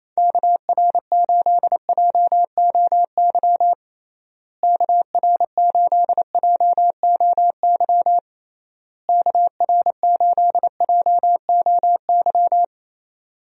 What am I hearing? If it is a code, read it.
Morse code, heard as a keyed tone, about 28 wpm, 700 Hz. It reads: KR8JOY KR8JOY KR8JOY
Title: USA Call Signs - 6 Characters - Repeated 3x - 28wpm